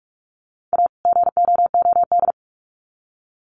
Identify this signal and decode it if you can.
Morse code, heard as a keyed tone, about 44 wpm, 700 Hz. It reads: A GOOD
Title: Sets of 2 Words 45wpm